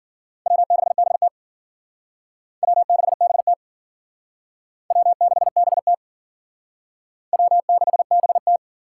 Morse code, heard as a keyed tone, about 50 wpm, 700 Hz. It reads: W6BT W6BT W6BT W6BT
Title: Speed Racing - USA 1x2 Call Signs 40wpm